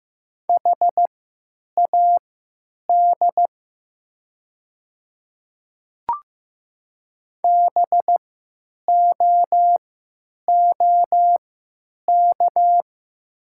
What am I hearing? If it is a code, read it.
Morse code, heard as a keyed tone, about 16 wpm, spaced as for 9 wpm, 700 Hz. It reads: HAD BOOK
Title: 3x Character Spacing - Top 300 Words - 15wpm